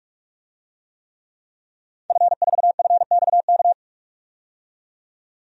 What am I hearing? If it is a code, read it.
Morse code, heard as a keyed tone, about 42 wpm, 700 Hz. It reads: F4FXK
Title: DXCC-100 Call Signs - 5 Characters - 45wpm